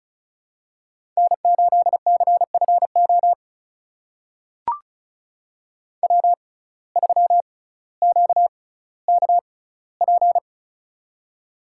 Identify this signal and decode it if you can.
Morse code, heard as a keyed tone, about 36 wpm, spaced as for 18 wpm, 700 Hz. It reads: N8CFO W3QKP
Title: USA Call Signs - 5 Characters - 6x Character Spacing - 35wpm